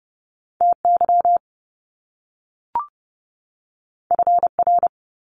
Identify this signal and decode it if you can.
Morse code, heard as a keyed tone, about 30 wpm, 700 Hz. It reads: TY FR
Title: Top Two Letter Combinations in Words 30wpm